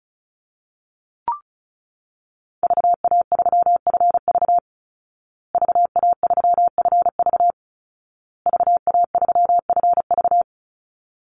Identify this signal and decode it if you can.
Morse code, heard as a keyed tone, about 34 wpm, 700 Hz. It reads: VA3FV VA3FV VA3FV
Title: Call Signs Repeated 2x - 35wpm